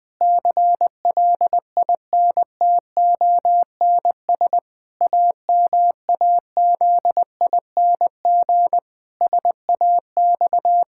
Morse code, heard as a keyed tone, about 20 wpm, 700 Hz. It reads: CLINTONS AMAZING SAX
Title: Pangram Challenge v2 - 20wpm